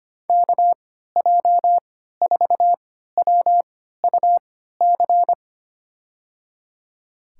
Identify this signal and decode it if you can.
Morse code, heard as a keyed tone, about 25 wpm, spaced as for 14 wpm, 700 Hz. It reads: KJ4WUC
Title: USA Call Signs - 6 Characters - 3x Character Spacing - 25wpm